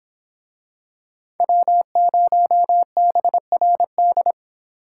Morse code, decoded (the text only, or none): W0BRD